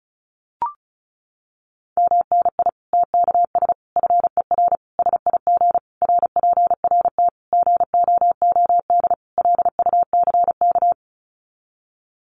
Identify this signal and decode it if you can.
Morse code, heard as a keyed tone, about 34 wpm, 700 Hz. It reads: MNI TKS FER SIG RPRT GOOD LUCK
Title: Common QSO Phrases 35wpm